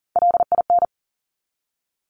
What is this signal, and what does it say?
Morse code, heard as a keyed tone, about 38 wpm, 690 Hz. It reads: LIN
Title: Top Three Letter Combinations in Words 40wpm